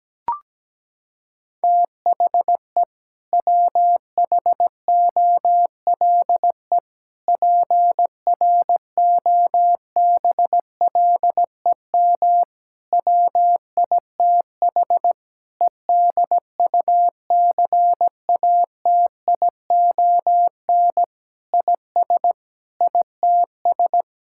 Morse code, heard as a keyed tone, about 17 wpm, 700 Hz. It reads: THE WHOLE PROBLEM WITH EDUCATION IS ITS
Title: Sentences from Top 1000 Words - Review Part 6 - 17wpm